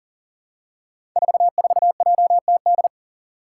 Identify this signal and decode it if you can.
Morse code, heard as a keyed tone, about 40 wpm, 700 Hz. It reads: 44JTD
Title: Five Challenging Letter-Number Combinations 40wpm